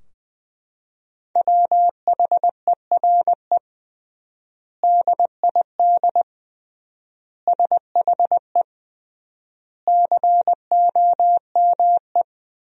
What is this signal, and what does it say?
Morse code, heard as a keyed tone, about 21 wpm, 700 Hz. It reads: WHERE DID SHE COME
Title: 3x Word-Spacing - Sentences from Top 200 Words - 20wpm